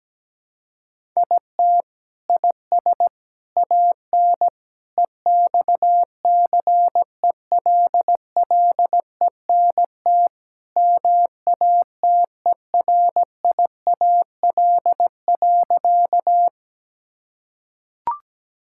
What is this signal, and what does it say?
Morse code, heard as a keyed tone, about 17 wpm, 700 Hz. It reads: IT IS AN EXCELLENT MATERIAL.
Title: Sentences from Top 1000 Words - Review Part 5 - 17wpm